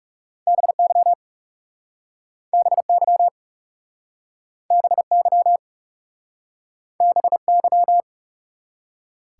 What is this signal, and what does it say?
Morse code, heard as a keyed tone, about 41 wpm, 700 Hz. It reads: BY BY BY BY